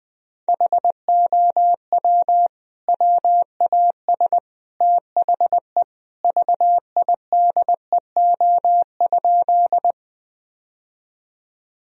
Morse code, heard as a keyed tone, about 21 wpm, 700 Hz. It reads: HOW WAS THE VIDEO?
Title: Sentences from Top 1000 Words - Review Part 4 - 20wpm